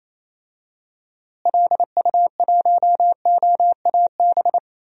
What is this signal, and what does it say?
Morse code, heard as a keyed tone, about 28 wpm, 700 Hz. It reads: LU1OAB
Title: DXCC-100 Call Signs - 6 Characters - Repeated 3x - 28wpm